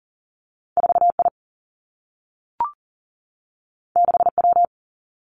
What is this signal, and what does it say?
Morse code, heard as a keyed tone, about 38 wpm, 700 Hz. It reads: 4I 6W
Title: Two Challenging Letter-Number Combinations 40wpm